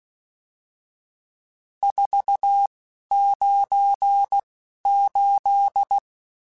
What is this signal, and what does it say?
Morse code, heard as a keyed tone, about 16 wpm, spaced as for 11 wpm, 800 Hz. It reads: 498
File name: RSTs / RST-mp3s / Various RST @10